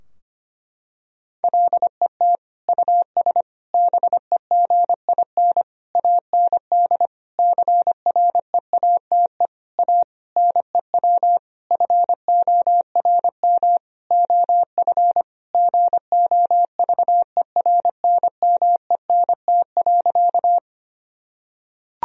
Morse code, heard as a keyed tone, about 25 wpm, 700 Hz. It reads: LET US BEGIN AND CREATE A NEW FORM OF GOVERNMENT.
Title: Sentences from Top 1000 Words - Review Part 3 - 25wpm